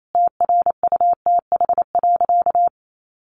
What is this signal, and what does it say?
Morse code, heard as a keyed tone, about 28 wpm, 700 Hz. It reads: TRUTH.
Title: Sentences from Top 1100 Words 28wpm